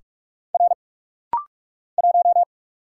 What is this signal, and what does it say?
Morse code, heard as a keyed tone, about 46 wpm, 700 Hz. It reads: R 1
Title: Single Letter-Number Alt - Rapid-Fire 45wpm